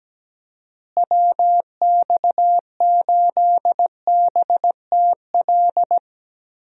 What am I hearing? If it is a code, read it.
Morse code, heard as a keyed tone, about 17 wpm, 700 Hz. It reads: WX8BTL
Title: USA Call Signs - 6 Characters - Repeated 3x - 17wpm